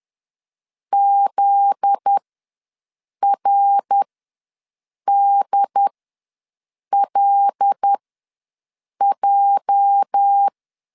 Morse code, heard as a keyed tone, about 10 wpm, spaced as for 6 wpm, 800 Hz. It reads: ZRDLJ
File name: LEZIONI / 8 YJ45 / 8 YJ ws 10wpm 4043